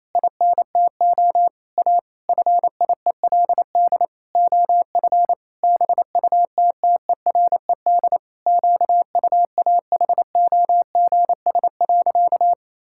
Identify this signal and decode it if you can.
Morse code, heard as a keyed tone, about 28 wpm, 700 Hz. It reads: INTO A FIELD OF BUTTERED QUAHOGS.
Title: Pangram Challenge v2 - 28wpm